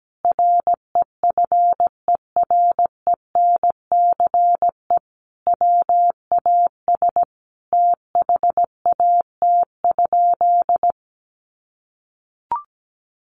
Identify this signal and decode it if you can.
Morse code, heard as a keyed tone, about 17 wpm, 700 Hz. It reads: REFERENCE WAS THAT?
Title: Sentences from Top 1200 Words 17wpm